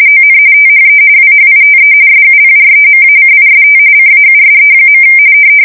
RTTY, about 45.45 baud. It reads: ON THE WHITE SUPREMACIST CAUSE.
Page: 170 Shift 2125/2295